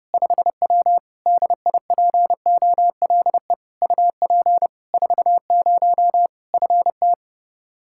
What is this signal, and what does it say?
Morse code, heard as a keyed tone, about 30 wpm, 700 Hz. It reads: 5W DIPOLE UP 40 FT
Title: Practice QSO - Exchange 2 - Info - 30wpm